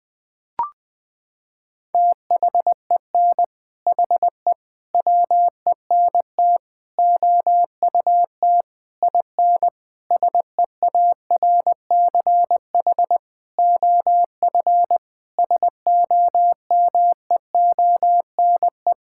Morse code, heard as a keyed tone, about 20 wpm, 700 Hz. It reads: THEN HE WENT OUT IN SEARCH OF SOMEONE
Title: Sentences from Top 1000 Words - Review Part 5 - 20wpm